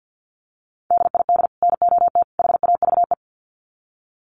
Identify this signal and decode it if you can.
Morse code, heard as a keyed tone, about 50 wpm, 700 Hz. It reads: DID NOT HAVE
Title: Sets of 3 Words 50wpm